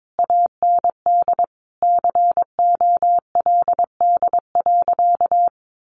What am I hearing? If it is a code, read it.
Morse code, heard as a keyed tone, about 22 wpm, 700 Hz. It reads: AND COLD.
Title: Sentences from Top 1500 Words 22wpm